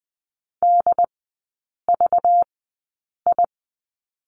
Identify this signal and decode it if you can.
Morse code, heard as a keyed tone, about 20 wpm, spaced as for 8 wpm, 700 Hz. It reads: DVI
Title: 191 Lesson 31e - Letter v - Three Characters 20wpm